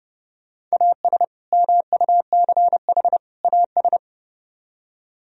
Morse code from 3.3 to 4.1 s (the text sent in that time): AS